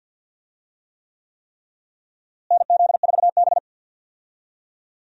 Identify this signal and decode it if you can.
Morse code, heard as a keyed tone, about 52 wpm, 700 Hz. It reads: NZ4B
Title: USA Call Signs - 4 Characters - Repeated 3x - 50wpm